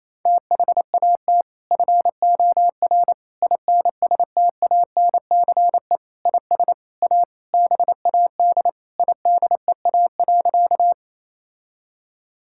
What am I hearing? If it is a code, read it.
Morse code, heard as a keyed tone, about 27 wpm, 700 Hz. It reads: THAT FOR INSTANCE IS A BAD IDEA.
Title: Sentences from Top 1000 Words - Review Part 5 - 28wpm